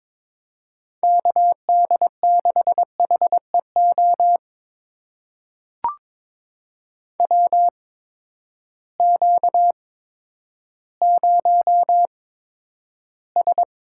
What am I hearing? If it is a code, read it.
Morse code, heard as a keyed tone, about 22 wpm, spaced as for 11 wpm, 700 Hz. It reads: KD6HEO WQ0S
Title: USA Call Signs - 8x Character Spacing - 22wpm